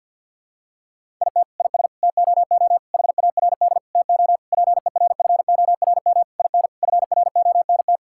Morse code, heard as a keyed tone, about 50 wpm, 700 Hz. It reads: IT IS TOO HARD TO PERFORM IN FRONT
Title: Sentences from Top 1000 Words - Review Part 5 - 50wpm